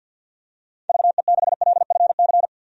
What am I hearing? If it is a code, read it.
Morse code, heard as a keyed tone, about 53 wpm, 700 Hz. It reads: VE6LFC
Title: DXCC-100 Call Signs - 6 Characters - 50wpm